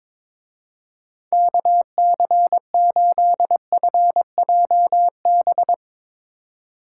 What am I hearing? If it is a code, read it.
Morse code, heard as a keyed tone, about 22 wpm, 700 Hz. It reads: KC8FJB
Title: Speed Racing - USA 2x3 Call Signs 22wpm